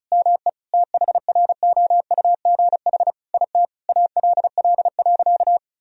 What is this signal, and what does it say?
Morse code, heard as a keyed tone, about 35 wpm, 700 Hz. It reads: ME THROUGH IT ALL.
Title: Sentences from Top 1000 Words - Review Part 1 - 35wpm